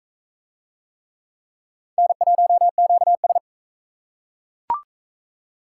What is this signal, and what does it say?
Morse code, heard as a keyed tone, about 36 wpm, 700 Hz. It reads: N1QS